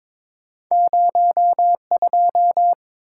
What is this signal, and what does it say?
Morse code, heard as a keyed tone, about 22 wpm, 700 Hz. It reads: 02
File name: practice / 22wpm / Two Letter-Number Combinations - 22wpm